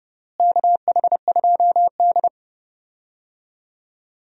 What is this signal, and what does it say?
Morse code, heard as a keyed tone, about 30 wpm, 700 Hz. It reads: KH2D
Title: USA Call Signs - 4 Characters - Repeated 2x - 30wpm